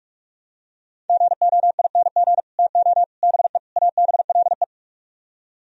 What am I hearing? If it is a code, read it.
Morse code, heard as a keyed tone, about 45 wpm, 700 Hz. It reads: GOING TO BE ABLE